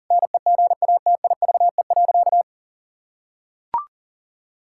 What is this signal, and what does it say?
Morse code, heard as a keyed tone, about 40 wpm, 700 Hz. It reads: NEGATIVE.
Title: Sentences from Top 1000 Words - Review Part 6 - 40wpm